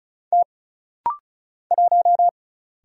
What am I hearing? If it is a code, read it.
Morse code, heard as a keyed tone, about 34 wpm, 700 Hz. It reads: T 1
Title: Single Letter-Number Alt - Rapid-Fire 35wpm